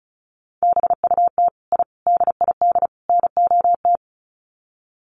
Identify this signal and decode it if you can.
Morse code, heard as a keyed tone, about 34 wpm, 700 Hz. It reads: BUT I DID NOT
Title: Sets of 4 Words 35wpm